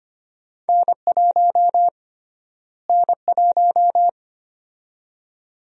Morse code, heard as a keyed tone, about 25 wpm, 700 Hz. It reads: N1 N1
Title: USA Call Sign Prefixes - Repeated 2x - 25wpm